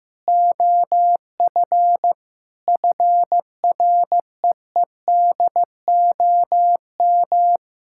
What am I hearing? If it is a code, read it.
Morse code, heard as a keyed tone, about 16 wpm, 700 Hz. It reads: OF FREEDOM
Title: Sentences from Top 1000 Words - Review Part 5 - 15wpm